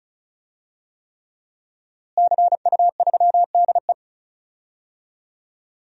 Morse code, heard as a keyed tone, about 34 wpm, 700 Hz. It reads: CU3DE
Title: DXCC-100 Call Signs - 5 Characters - Repeated 3x - 35wpm